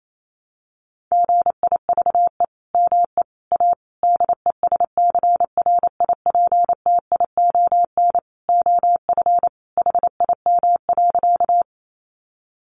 Morse code, heard as a keyed tone, about 28 wpm, 700 Hz. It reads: GIVE ME A DESCRIPTION OF HIM.
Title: Sentences from Top 1300 Words 28wpm